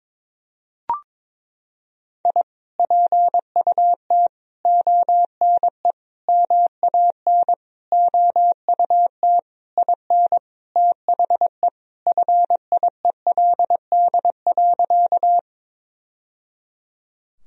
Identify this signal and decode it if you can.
Morse code, heard as a keyed tone, about 22 wpm, 700 Hz. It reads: I PUT ONE MAN OUT IN THE FIELD.
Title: Sentences from Top 1000 Words - Review Part 2 - 22wpm